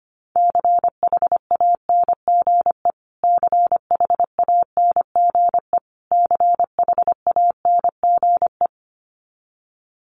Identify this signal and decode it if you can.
Morse code, heard as a keyed tone, about 25 wpm, 700 Hz. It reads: CHANGE CHANGE CHANGE
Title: Top 500 Words Repeated 25wpm